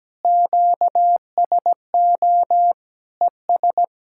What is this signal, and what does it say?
Morse code, heard as a keyed tone, about 17 wpm, 700 Hz. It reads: QSO ES
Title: Practice QSO - Exchange 4 - Ending - 17wpm